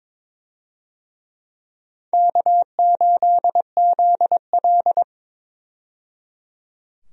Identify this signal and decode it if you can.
Morse code, heard as a keyed tone, about 22 wpm, 700 Hz. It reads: K8ZL
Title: USA Call Signs - 4 Characters - 22wpm